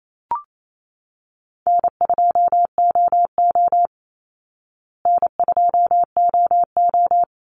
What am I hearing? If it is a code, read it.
Morse code, heard as a keyed tone, about 28 wpm, 700 Hz. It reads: N2OO N2OO
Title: Easy Call Signs Repeated 2x - 28wpm